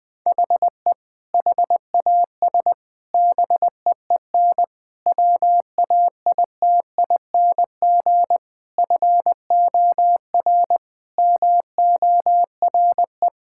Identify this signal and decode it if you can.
Morse code, heard as a keyed tone, about 21 wpm, 700 Hz. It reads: HE HAS BEEN WAITING FOR MORE
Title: Sentences with 3-Word Phrases - 20wpm